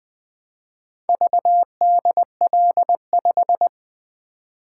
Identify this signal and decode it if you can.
Morse code, heard as a keyed tone, about 20 wpm, 700 Hz. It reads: VDL5
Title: Four Letter-Number Combinations - 20wpm